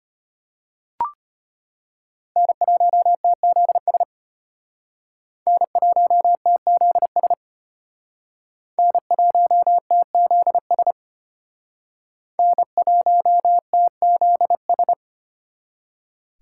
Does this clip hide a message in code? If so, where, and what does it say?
Yes: Morse code from 2.4 to 14.9 s, heard as a keyed tone, about 33 wpm, 700 Hz. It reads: N1TZS N1TZS N1TZS N1TZS